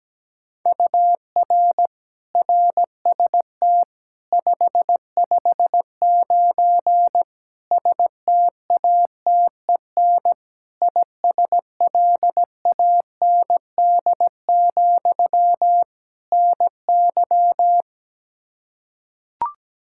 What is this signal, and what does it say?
Morse code, heard as a keyed tone, about 17 wpm, 700 Hz. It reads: UR RST 559 STATEN ISLAND, NY
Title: Practice QSO - Exchange 1 - Beginning - 17wpm